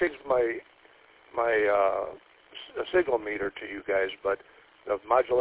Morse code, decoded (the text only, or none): none